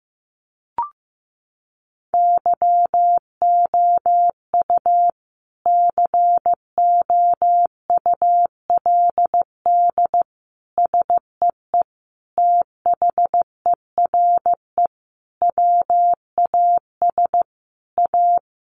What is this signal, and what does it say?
Morse code, heard as a keyed tone, about 14 wpm, 700 Hz. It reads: YOU COULD SEE THERE WAS A
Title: Sentences from Top 500 Words 15wpm